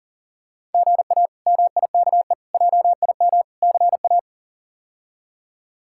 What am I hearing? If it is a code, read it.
Morse code, heard as a keyed tone, about 37 wpm, 700 Hz. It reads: GA MIKE JIM CA
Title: K1USN SST QSO Practice - 40wpm